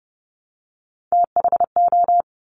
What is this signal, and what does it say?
Morse code, heard as a keyed tone, about 30 wpm, 690 Hz. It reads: THO
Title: Top Three Letter Combinations in Words 30wpm